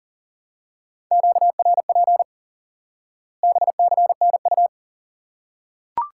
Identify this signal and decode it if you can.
Morse code, heard as a keyed tone, about 40 wpm, 700 Hz. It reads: QRP BCNU
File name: IQE Course 3 - Lesson 89 - YAGI - Two Elements - 40wpm